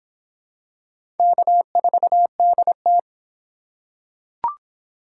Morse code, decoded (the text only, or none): K4DT